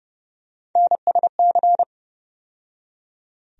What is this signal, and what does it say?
Morse code, heard as a keyed tone, about 30 wpm, 700 Hz. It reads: NSC
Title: USA Call Sign Postfixes - 3 Letters 30wpm